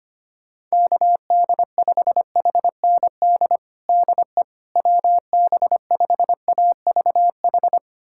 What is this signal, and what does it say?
Morse code, heard as a keyed tone, about 25 wpm, 700 Hz. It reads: KD5HND DE WB5AVH
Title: USA Callsign DE Callsign - 6 Characters - 25wpm